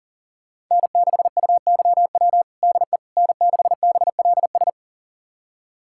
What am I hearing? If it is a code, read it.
Morse code, heard as a keyed tone, about 40 wpm, 700 Hz. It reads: N6UYW DE N6BLS
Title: USA Callsign DE Callsign - 5 Characters - 40wpm